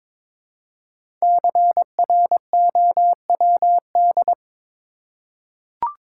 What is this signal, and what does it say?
Morse code, heard as a keyed tone, about 22 wpm, 700 Hz. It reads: CROWD
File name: 5 Letter Words - First 3 Letters Given - 22wpm